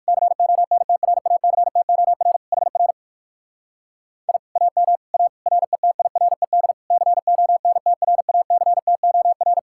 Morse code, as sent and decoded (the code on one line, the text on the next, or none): -.-. --- -. - .-. .- -.-. - --- .-. / .... .-. / .. / .- -- / .- / .-. - .. .-. -.. / -.-. --- -. - .-. .- -.-. - --- .-.
CONTRACTOR HR I AM A RETIRED CONTRACTOR